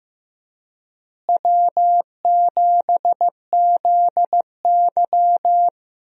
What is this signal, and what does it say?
Morse code, heard as a keyed tone, about 15 wpm, 700 Hz. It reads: W7ZY
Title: USA Call Signs - 4 Characters - Repeated 3x - 15wpm